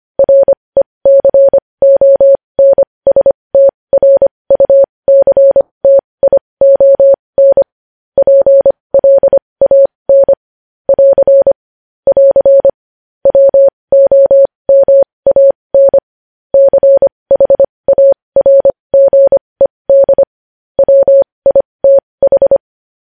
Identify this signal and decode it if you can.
Morse code, heard as a keyed tone, about 25 wpm, spaced as for 20 wpm, 550 Hz. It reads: RECONSTRUCTION PLAN <AR> <AR> WOMAN CHARGED WITH
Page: News Headlines in Morse Code at 20 WPM